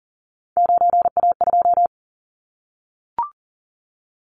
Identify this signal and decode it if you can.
Morse code, heard as a keyed tone, about 40 wpm, 700 Hz. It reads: 9A2
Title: Callsign Prefixes for Top 25 DXCC - Repeated 3x - 40wpm